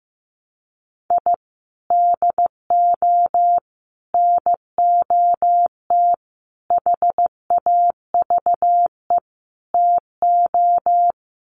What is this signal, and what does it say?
Morse code, heard as a keyed tone, about 15 wpm, 700 Hz. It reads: I DO NOT HAVE TO
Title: Sets of 5 Words 15wpm